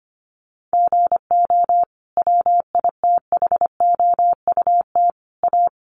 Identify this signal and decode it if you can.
Morse code, heard as a keyed tone, about 25 wpm, 700 Hz. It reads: GO WITHOUT A
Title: Sentences from Top 1300 Words 25wpm